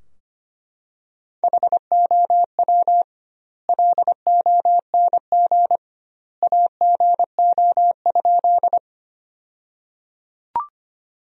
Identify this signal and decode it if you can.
Morse code, heard as a keyed tone, about 25 wpm, 700 Hz. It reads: HOW LONG AGO?
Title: 2x Word-Spacing - Sentences from Top 500 Words - 25wpm